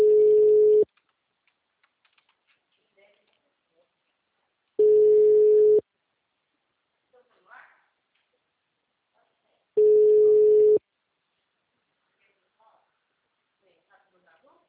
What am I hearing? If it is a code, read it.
Ringback tone (Europe, 425 Hz), with speech.